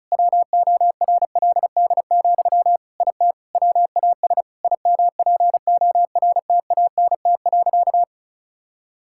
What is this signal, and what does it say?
Morse code, heard as a keyed tone, about 36 wpm, 700 Hz. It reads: WORLD, IT WAS IMPORTANT.
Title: Sentences from Top 1000 Words - Review Part 5 - 35wpm